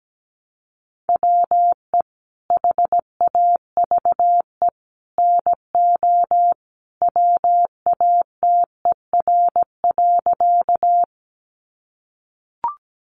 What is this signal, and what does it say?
Morse code, heard as a keyed tone, about 17 wpm, 700 Hz. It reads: WE HAVE NO WATER.
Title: Sentences from Top 100 Words 17wpm